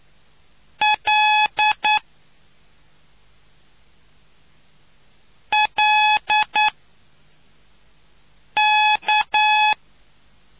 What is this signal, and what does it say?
Morse code, heard as a keyed tone, about 9 wpm, 850 Hz. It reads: L L K